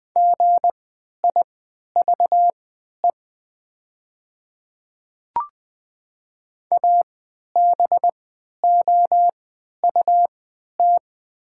Morse code, heard as a keyed tone, about 20 wpm, spaced as for 11 wpm, 700 Hz. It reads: GIVE ABOUT
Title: 3x Character Spacing - Top 300 Words - 20wpm